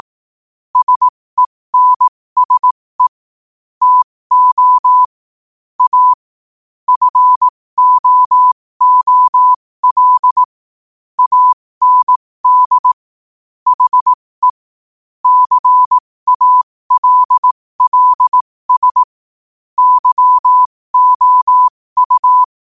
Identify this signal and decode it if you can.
Morse code, heard as a keyed tone, about 18 wpm, spaced as for 15 wpm, 1000 Hz. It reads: SENSE TO A FOOL AND HE CALLS YOU
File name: qotdfiles / 15wpm